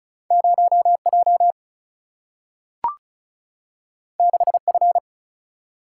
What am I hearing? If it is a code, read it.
Morse code, heard as a keyed tone, about 36 wpm, 700 Hz. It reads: 0 J 6 F
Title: Single-Letter Single-Number 35wpm